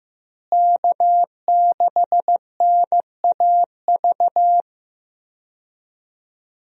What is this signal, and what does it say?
Morse code, heard as a keyed tone, about 15 wpm, 700 Hz. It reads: K6NAV
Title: USA Call Signs - 5 Characters - Repeated 3x - 15wpm